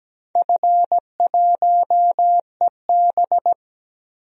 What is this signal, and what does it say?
Morse code, heard as a keyed tone, about 17 wpm, 700 Hz. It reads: F1EB
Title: DXCC-100 Call Signs - 4 Characters - 17wpm